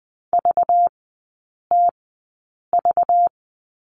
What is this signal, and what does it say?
Morse code, heard as a keyed tone, about 20 wpm, spaced as for 8 wpm, 700 Hz. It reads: VTV
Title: 191 Lesson 31e - Letter v - Three Characters 20wpm